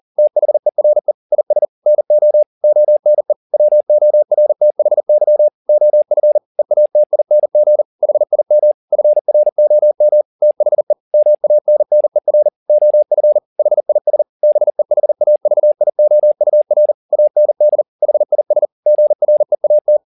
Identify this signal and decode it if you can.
Morse code, heard as a keyed tone, about 40 wpm, 600 Hz. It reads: THERE IS NO ONE WORTHY OF EATING HIM FROM THE MANNER OF HIS BEHAVIOUR AND HIS GREAT
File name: the old man and the sea 2 40wpm